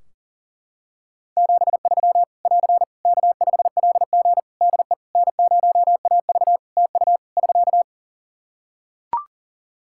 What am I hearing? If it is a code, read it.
Morse code, heard as a keyed tone, about 37 wpm, 700 Hz. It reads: 73 <AR> K5LG DE N0AV TU <SK>
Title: K1USN SST QSO Practice 40wpm